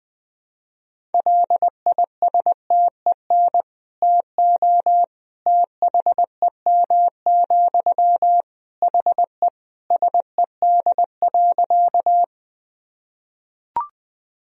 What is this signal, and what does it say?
Morse code, heard as a keyed tone, about 20 wpm, 700 Hz. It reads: LISTEN TO THEM, HE SED.
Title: Sentences from Top 400 Words and Abbreviations 20wpm